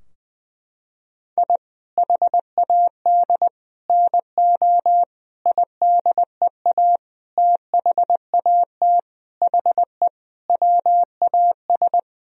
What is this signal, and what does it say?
Morse code, heard as a keyed tone, about 20 wpm, 700 Hz. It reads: I HAD NO IDEA THAT HE WAS
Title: Sentences from Top 1000 Words - Review Part 2 - 20wpm